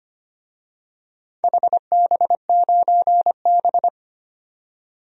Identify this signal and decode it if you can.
Morse code, heard as a keyed tone, about 25 wpm, 700 Hz. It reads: HB9B
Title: DXCC-100 Call Signs - 4 Characters - 25wpm